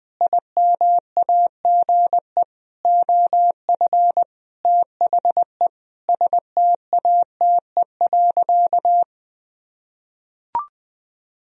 Morse code, heard as a keyed tone, about 20 wpm, 700 Hz. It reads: IMAGE OF THE STATE.
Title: Sentences from Top 1000 Words - Review Part 4 - 20wpm